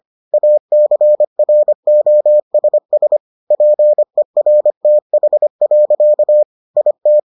Morse code, heard as a keyed tone, about 25 wpm, 600 Hz. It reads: ACROSS PERTH. IT